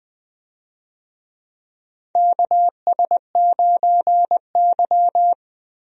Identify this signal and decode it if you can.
Morse code, heard as a keyed tone, about 21 wpm, 700 Hz. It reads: KS9Y
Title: USA Call Signs - 4 Characters - Repeated 3x - 20wpm